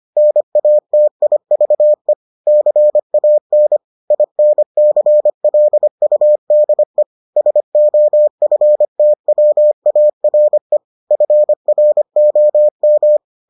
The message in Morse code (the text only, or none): NATIVE CAN INCLUDE SOFTWARE FROM